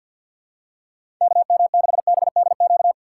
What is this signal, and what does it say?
Morse code, heard as a keyed tone, about 52 wpm, 700 Hz. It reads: KM6BDQ
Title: USA Call Signs - 6 Characters - 6x Character Spacing - 50wpm